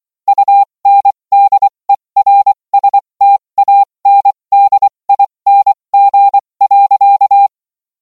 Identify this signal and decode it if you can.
Morse code, heard as a keyed tone, about 24 wpm, spaced as for 20 wpm, 800 Hz. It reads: UNDERSTANDING.